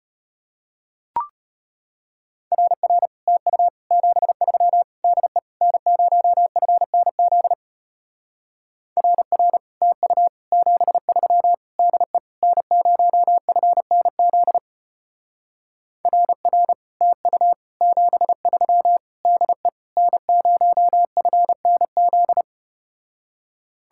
Morse code, heard as a keyed tone, about 33 wpm, 700 Hz. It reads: RR TU 73 DE N0FNZ RR TU 73 DE N0FNZ RR TU 73 DE N0FNZ